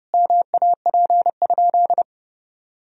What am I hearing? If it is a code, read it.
Morse code, heard as a keyed tone, about 30 wpm, 700 Hz. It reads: MAP?
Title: Sentences from Top 1000 Words - Review Part 2 - 30wpm